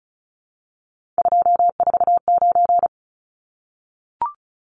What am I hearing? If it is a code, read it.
Morse code, heard as a keyed tone, about 34 wpm, 700 Hz. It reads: 249
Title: Numbers - 3-Digit 35wpm